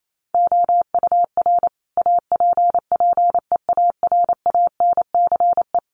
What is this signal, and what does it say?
Morse code, heard as a keyed tone, about 28 wpm, 700 Hz. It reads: OUR APPEARANCE
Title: Sentences from Top 1100 Words 28wpm